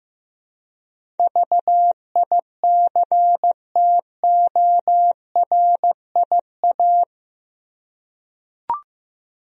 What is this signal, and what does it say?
Morse code, heard as a keyed tone, about 15 wpm, 700 Hz. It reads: VICTORIA
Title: USA Common Names - 15wpm